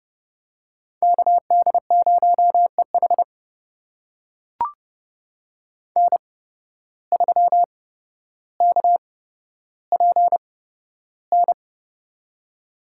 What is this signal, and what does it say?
Morse code, heard as a keyed tone, about 32 wpm, spaced as for 13 wpm, 700 Hz. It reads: KD0EH N3KPN
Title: USA Call Signs - 5 Characters - 8x Character Spacing - 30wpm